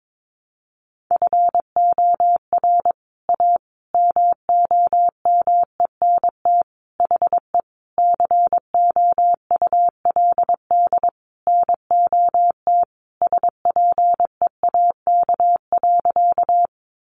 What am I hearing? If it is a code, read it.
Morse code, heard as a keyed tone, about 22 wpm, 700 Hz. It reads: FOR A MOMENT HE COULD NOT SPEAK.
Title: Sentences from Top 1100 Words 22wpm